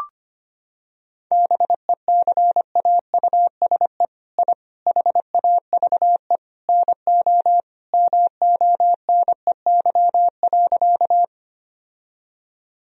Morse code, heard as a keyed tone, about 25 wpm, 700 Hz. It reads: BECAUSE I HAVE NO MONEY.
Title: Sentences from Top 1000 Words - Review Part 3 - 25wpm